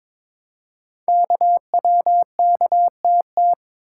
Morse code, heard as a keyed tone, about 22 wpm, 700 Hz. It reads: KWKTT
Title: Code Groups - 5 Letters - 22wpm